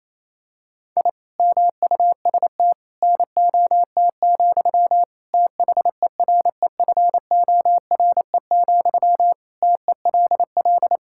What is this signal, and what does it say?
Morse code, heard as a keyed tone, about 28 wpm, 700 Hz. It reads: I MUST NOT, THEREFORE, TELL
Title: Sentences from Top 1000 Words - Review Part 5 - 28wpm